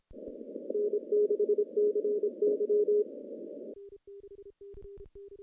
Morse code, heard as a keyed tone, about 29 wpm, 410 Hz. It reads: N6CY
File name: contests / CW-SS-k5tr- -0100